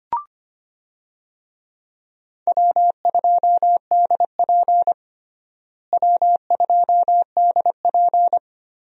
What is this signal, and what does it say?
Morse code, heard as a keyed tone, about 25 wpm, 700 Hz. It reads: W2DP W2DP
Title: USA Call Signs - 4 Characters - Repeated 3x - 25wpm